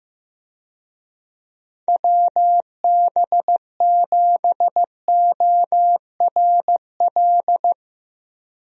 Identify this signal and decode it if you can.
Morse code, heard as a keyed tone, about 15 wpm, 700 Hz. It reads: WB7ORL
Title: USA Call Signs - 6 Characters - Repeated 3x - 15wpm